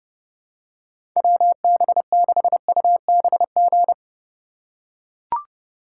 Morse code, heard as a keyed tone, about 30 wpm, 700 Hz. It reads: WB6UBG
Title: USA Call Signs - 6x Character Spacing - 30wpm